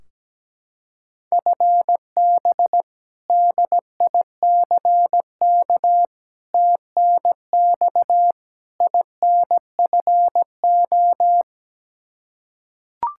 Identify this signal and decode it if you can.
Morse code, heard as a keyed tone, about 17 wpm, 700 Hz. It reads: FB DICK TNX INFO
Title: Practice QSO - Exchange 3 - More Info - 17wpm